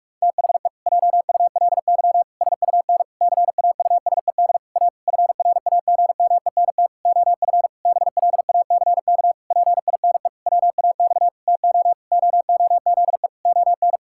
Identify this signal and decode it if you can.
Morse code, heard as a keyed tone, about 46 wpm, 700 Hz. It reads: THE JULY SUN CAUSED A FRAGMENT OF BLACK PINE WAX TO OOZE ON
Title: Pangram Challenge v2 - 45wpm